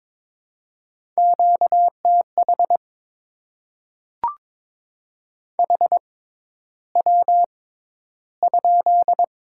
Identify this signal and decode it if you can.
Morse code, heard as a keyed tone, about 22 wpm, spaced as for 11 wpm, 690 Hz. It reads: QTH HW?
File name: Top 20 QSO Elements - 6x Character Spacing - 22wpm